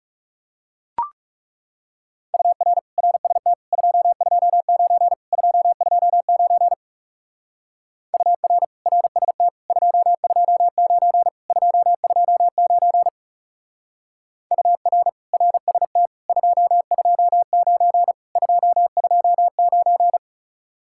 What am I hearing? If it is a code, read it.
Morse code, heard as a keyed tone, about 40 wpm, 700 Hz. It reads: UR RST 229 229 UR RST 229 229 UR RST 229 229